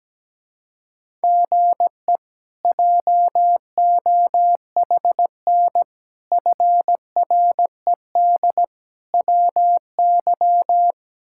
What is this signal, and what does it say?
Morse code, heard as a keyed tone, about 17 wpm, 700 Hz. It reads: GE JOHN FRED WY
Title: K1USN SST QSO Practice - 17wpm